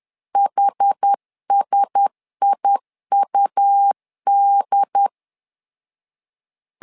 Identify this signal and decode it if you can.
Morse code, heard as a keyed tone, about 10 wpm, 800 Hz. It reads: HSIUD
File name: LEZIONI / 5 LBVC / 5 L 10wpm 2259